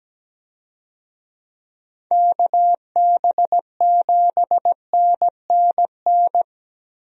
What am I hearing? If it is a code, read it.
Morse code, heard as a keyed tone, about 17 wpm, 700 Hz. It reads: KB7NNN
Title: USA Call Signs - 6 Characters - Repeated 2x - 17wpm